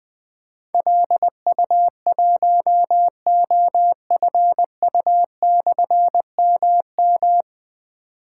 Morse code, heard as a keyed tone, about 20 wpm, 700 Hz. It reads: LU1OFU/MM
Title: Difficult Call Signs 20wpm